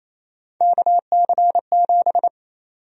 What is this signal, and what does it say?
Morse code, heard as a keyed tone, about 28 wpm, 700 Hz. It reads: KC7
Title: USA Call Sign Prefixes - 28wpm